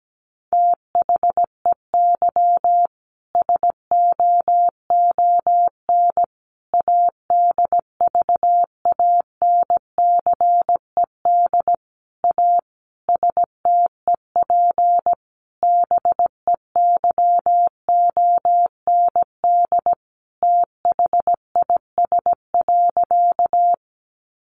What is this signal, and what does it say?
Morse code, heard as a keyed tone, about 17 wpm, 700 Hz. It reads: THEY SOON ADVANCED A STEP BEYOND THIS.
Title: Sentences from Top 1200 Words 17wpm